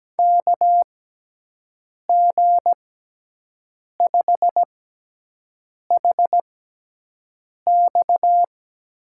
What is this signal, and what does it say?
Morse code, heard as a keyed tone, about 17 wpm, spaced as for 6 wpm, 700 Hz. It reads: KG5HX
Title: USA Call Signs - 6x Character Spacing - 17wpm